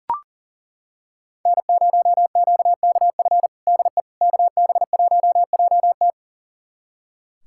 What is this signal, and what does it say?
Morse code, heard as a keyed tone, about 37 wpm, 700 Hz. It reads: N0QKF DE KB1JT
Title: USA Callsign DE Callsign - 5 Characters - 40wpm